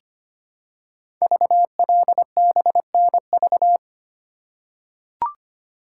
Morse code, heard as a keyed tone, about 25 wpm, 700 Hz. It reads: VLBNV
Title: Code Groups - 5 Letter:Numbers - 25wpm